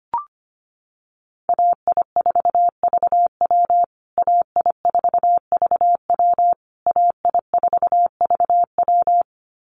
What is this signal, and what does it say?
Morse code, heard as a keyed tone, about 25 wpm, 700 Hz. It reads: AI4VW AI4VW AI4VW
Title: USA Call Signs Repeated 3x 25wpm